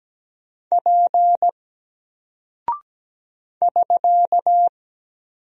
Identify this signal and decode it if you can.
Morse code, heard as a keyed tone, about 17 wpm, 700 Hz. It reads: P <SK>
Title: Single Character (Letter:Number:Punctuation:Prosign) Alt 17wpm